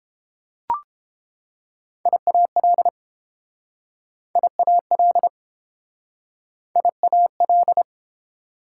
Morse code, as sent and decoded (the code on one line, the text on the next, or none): .. .- .-.. / .. .- .-.. / .. .- .-..
IAL IAL IAL